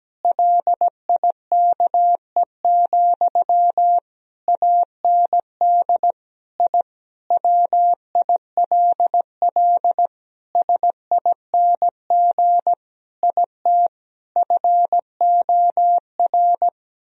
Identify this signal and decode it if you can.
Morse code, heard as a keyed tone, about 17 wpm, 700 Hz. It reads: LIKE, AND I WILL SING IT FOR
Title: Sentences from Top 1000 Words - Review Part 2 - 17wpm